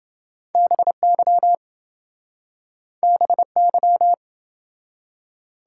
Morse code, heard as a keyed tone, about 28 wpm, 700 Hz. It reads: BY BY